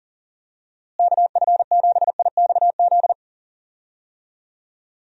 Morse code, heard as a keyed tone, about 39 wpm, 700 Hz. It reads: KF7IXZ
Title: USA Call Signs - 6 Characters - Repeated 2x - 40wpm